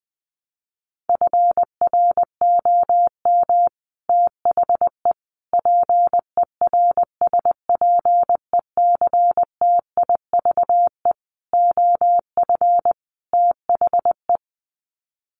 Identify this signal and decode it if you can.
Morse code, heard as a keyed tone, about 20 wpm, 700 Hz. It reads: FROM THE PERSPECTIVE OF THE
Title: Sets of 5 Words Encore2 20wpm